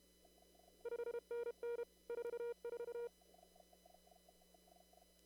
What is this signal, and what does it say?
Morse code, heard as a keyed tone, about 31 wpm, 470 Hz. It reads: 5NN 44
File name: CW / kt5j wpx cw-op-k5pi